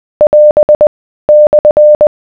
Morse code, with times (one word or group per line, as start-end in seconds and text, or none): 0.2-0.9 <AS>
1.3-2.1 /